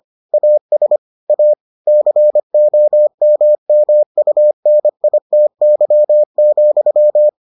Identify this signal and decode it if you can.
Morse code, heard as a keyed tone, about 25 wpm, 600 Hz. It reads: AS A COMMUNITY,